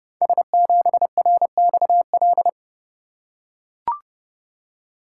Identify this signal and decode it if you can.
Morse code, heard as a keyed tone, about 30 wpm, 700 Hz. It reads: S7RXL
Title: Code Groups - 5 Letter:Numbers - 30wpm